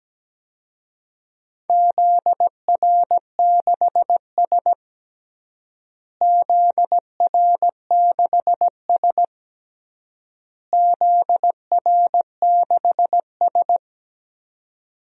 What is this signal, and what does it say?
Morse code, heard as a keyed tone, about 17 wpm, 700 Hz. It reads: ZR6S ZR6S ZR6S
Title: DXCC-100 Call Signs - 4 Characters - Repeated 3x - 17wpm